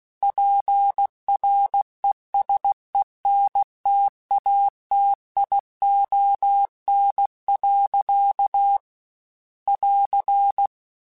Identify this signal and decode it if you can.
Morse code, heard as a keyed tone, about 16 wpm, 800 Hz. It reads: PRESENTATION. <AR>
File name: Quotes / Quotes-mp3s / Persperation @15